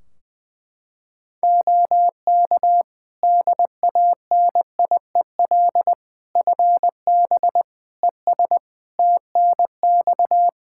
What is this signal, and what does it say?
Morse code, heard as a keyed tone, about 21 wpm, 700 Hz. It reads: OK DANIEL FB ES TNX
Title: Practice QSO - Exchange 2 - Info - 20wpm